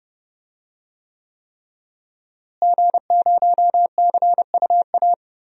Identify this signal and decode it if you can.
Morse code, heard as a keyed tone, about 30 wpm, 700 Hz. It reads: G0CUA
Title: DXCC-100 Call Signs - 5 Characters - Repeated 3x - 30wpm